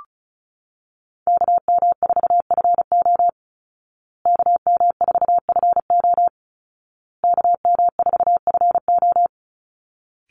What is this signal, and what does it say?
Morse code, heard as a keyed tone, about 34 wpm, 700 Hz. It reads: KM4FO KM4FO KM4FO
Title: Call Signs Repeated 2x - 35wpm